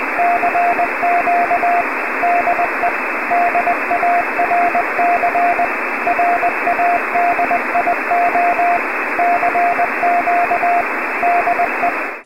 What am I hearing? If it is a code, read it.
Morse code, heard as a keyed tone, about 20 wpm, 690 Hz, with radio noise. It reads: CQ DE DARC RADIO CQ DE